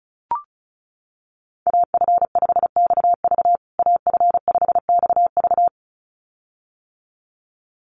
Morse code, heard as a keyed tone, about 34 wpm, 700 Hz. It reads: AF5XV AF5XV
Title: USA Call Signs Repeated 2x 35wpm